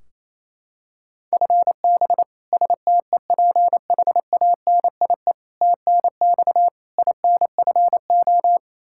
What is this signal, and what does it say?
Morse code, heard as a keyed tone, about 28 wpm, 700 Hz. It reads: FB STEPHANIE TNX INFO
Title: Practice QSO - Exchange 3 - More Info - 28wpm